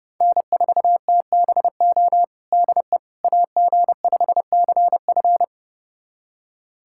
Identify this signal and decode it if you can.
Morse code, heard as a keyed tone, about 30 wpm, 700 Hz. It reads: N4TBO DE AG5CF
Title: USA Callsign DE Callsign - 5 Characters - 30wpm